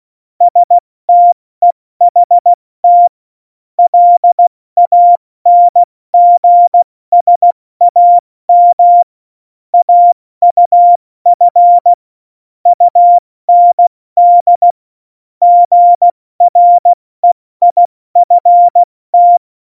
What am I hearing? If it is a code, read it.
Morse code, heard as a keyed tone, about 16 wpm, 700 Hz. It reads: STEHT LANGSAM AUF UND GREIFT
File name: Morsekurs / Witze / Witz 22. Dezember 2020, 70 F80